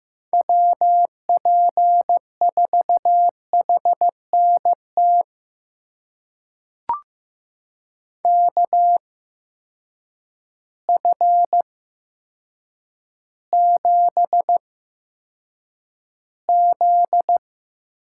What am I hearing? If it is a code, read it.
Morse code, heard as a keyed tone, about 15 wpm, spaced as for 8 wpm, 700 Hz. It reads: WP4HNT KF7Z
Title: USA Call Signs - 6 Characters - 8x Character Spacing - 15wpm